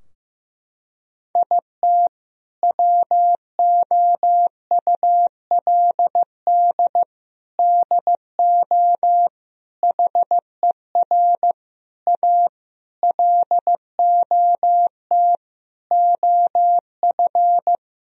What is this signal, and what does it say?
Morse code, heard as a keyed tone, about 15 wpm, 700 Hz. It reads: IT WOULD DO HER A LOT OF
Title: Sentences from Top 1000 Words - Review Part 3 - 15wpm